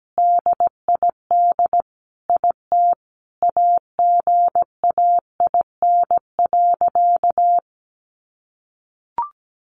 Morse code, heard as a keyed tone, about 17 wpm, 700 Hz. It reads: DID IT AGAIN.
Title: Sentences from Top 300 Words 17wpm